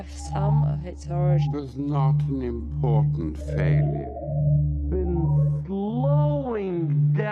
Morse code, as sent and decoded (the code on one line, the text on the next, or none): none